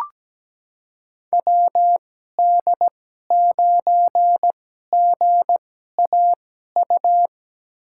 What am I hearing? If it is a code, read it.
Morse code, heard as a keyed tone, about 17 wpm, spaced as for 12 wpm, 700 Hz. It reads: WD9GAU
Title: USA Call Signs - 6 Characters - 2x Character Spacing - 17wpm